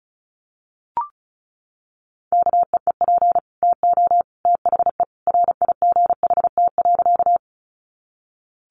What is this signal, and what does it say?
Morse code, heard as a keyed tone, about 34 wpm, 700 Hz. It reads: KEEP TO THE RIGHT.
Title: Sentences from Top 300 Words 35wpm